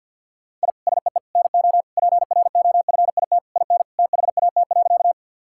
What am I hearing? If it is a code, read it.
Morse code, heard as a keyed tone, about 52 wpm, 700 Hz. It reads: I SEE NO PROFIT IN THAT.
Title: Sentences from Top 1000 Words - Review Part 5 - 50wpm